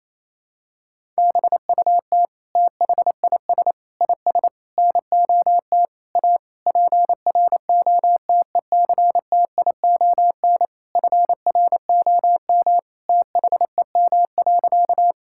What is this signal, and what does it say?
Morse code, heard as a keyed tone, about 28 wpm, 700 Hz. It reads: BUT THIS IS NOT A PROTECTION FROM THEM.
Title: Sentences from Top 1600 Words 28wpm